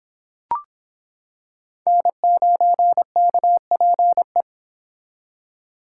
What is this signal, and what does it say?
Morse code, heard as a keyed tone, about 25 wpm, 700 Hz. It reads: N9KPE